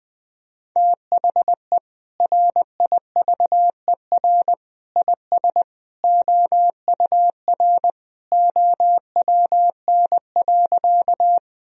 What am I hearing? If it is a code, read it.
Morse code, heard as a keyed tone, about 20 wpm, 700 Hz. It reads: THE RIVER IS OUR OWN.
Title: Sentences from Top 1000 Words - Review Part 1 - 20wpm